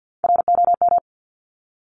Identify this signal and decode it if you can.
Morse code, heard as a keyed tone, about 50 wpm, 700 Hz. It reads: ROM
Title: Top Three Letter Combinations in Words 50wpm